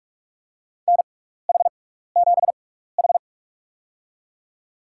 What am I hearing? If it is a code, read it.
Morse code, heard as a keyed tone, about 45 wpm, spaced as for 15 wpm, 700 Hz. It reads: NH7H
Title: USA Call Signs - 4 Characters - 6x Character Spacing - 45wpm